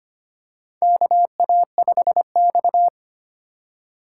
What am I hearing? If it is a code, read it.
Morse code, heard as a keyed tone, about 25 wpm, 700 Hz. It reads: KA5X
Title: USA Call Signs - 3x Character Spacing - 25wpm